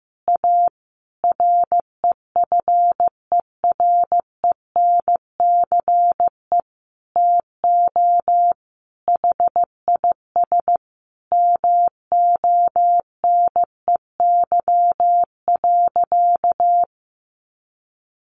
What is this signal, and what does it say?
Morse code, heard as a keyed tone, about 15 wpm, 700 Hz. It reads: A REFERENCE TO HIS MONEY.
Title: Sentences from Top 1200 Words 15wpm